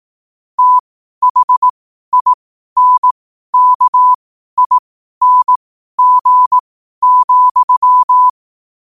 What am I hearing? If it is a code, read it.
Morse code, heard as a keyed tone, about 18 wpm, spaced as for 12 wpm, 1000 Hz. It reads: THINKING,
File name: qotdfiles / 12wpm